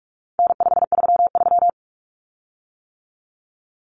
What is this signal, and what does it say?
Morse code, heard as a keyed tone, about 44 wpm, 700 Hz. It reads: N533
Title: Common Cut Numbers - 4-Digit 45wpm